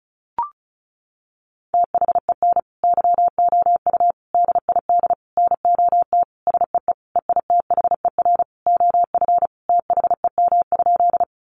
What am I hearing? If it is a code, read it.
Morse code, heard as a keyed tone, about 34 wpm, 700 Hz. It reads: THEN YOU DID NOT SEE EITHER OF THEM?
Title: Sentences from Top 600 Words 35wpm